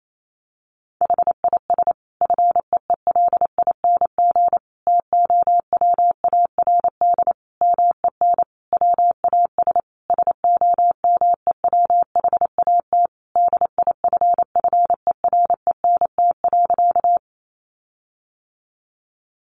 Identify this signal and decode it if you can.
Morse code, heard as a keyed tone, about 28 wpm, 700 Hz. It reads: HIS FEELING TOWARD MEN WAS SOMEWHAT DIFFERENT.
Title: Sentences from Top 1300 Words 28wpm